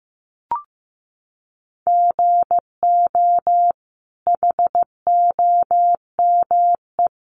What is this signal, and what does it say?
Morse code, heard as a keyed tone, about 15 wpm, 700 Hz. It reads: GO HOME
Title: Sentences from Top 800 Words 15wpm